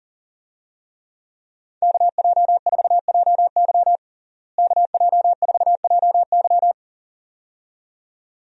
Morse code, heard as a keyed tone, about 40 wpm, 700 Hz. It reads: KJ4JY KJ4JY
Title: USA Call Signs - 5 Characters - Repeated 2x - 40wpm